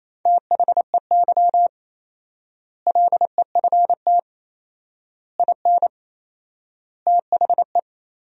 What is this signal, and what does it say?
Morse code, heard as a keyed tone, about 28 wpm, 700 Hz. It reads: THEY LEFT IN THE
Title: 4x Word-Spacing - Sentences from Top 400 Words - 28wpm